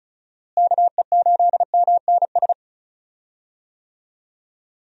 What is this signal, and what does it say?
Morse code, heard as a keyed tone, about 36 wpm, 700 Hz. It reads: KE8MNS
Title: USA Call Signs - 6 Characters - Repeated 3x - 35wpm